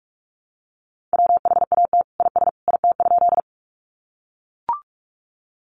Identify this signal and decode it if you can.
Morse code, heard as a keyed tone, about 44 wpm, 700 Hz. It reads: WHAT IS IT?
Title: Sentences from Top 100 Words 45wpm